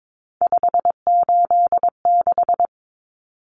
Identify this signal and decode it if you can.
Morse code, heard as a keyed tone, about 22 wpm, 700 Hz. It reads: 586
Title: Common Cut Numbers - 3-Digit 22wpm